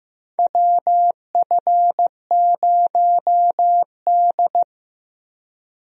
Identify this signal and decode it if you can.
Morse code, heard as a keyed tone, about 15 wpm, 700 Hz. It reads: WF0D
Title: USA Call Signs - 4 Characters - Repeated 3x - 15wpm